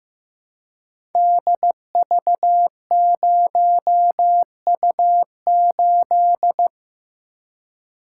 Morse code, heard as a keyed tone, about 15 wpm, 700 Hz. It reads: DV0U8
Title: Code Groups - 5 Letter:Numbers - 15wpm